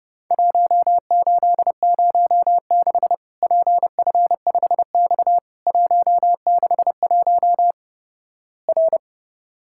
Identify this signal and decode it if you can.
Morse code, heard as a keyed tone, about 32 wpm, 700 Hz. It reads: 1806 PF5X 161 R
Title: Worked All Europe DX Contest - QTC Exchange - 30wpm